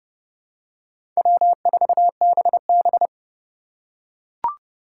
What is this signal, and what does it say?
Morse code, heard as a keyed tone, about 31 wpm, 700 Hz. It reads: W4BB
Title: USA Call Signs - 4 Characters - 3x Character Spacing - 30wpm